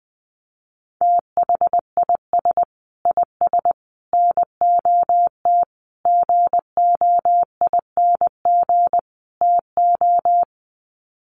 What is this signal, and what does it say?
Morse code, heard as a keyed tone, about 20 wpm, 700 Hz. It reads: THIS IS NOT GOING TO
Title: Sets of 5 Words Encore2 20wpm